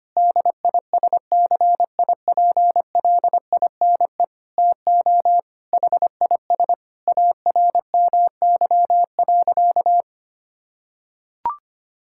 Morse code, heard as a keyed tone, about 25 wpm, 700 Hz. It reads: DISCIPLINE TO HIS ARMY.
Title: Sentences from Top 1000 Words - Review Part 6 - 25wpm